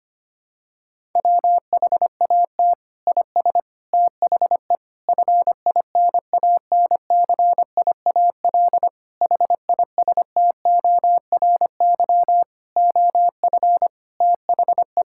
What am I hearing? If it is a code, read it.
Morse code, heard as a keyed tone, about 25 wpm, 700 Hz. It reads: WHAT IS THE FINANCIAL HISTORY OF THE
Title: Sentences from Top 1000 Words - Review Part 5 - 25wpm